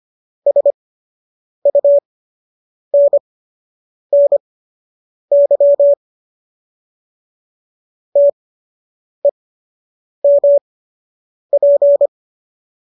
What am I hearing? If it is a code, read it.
Morse code, heard as a keyed tone, about 25 wpm, spaced as for 8 wpm, 570 Hz. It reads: SUNNY TEMP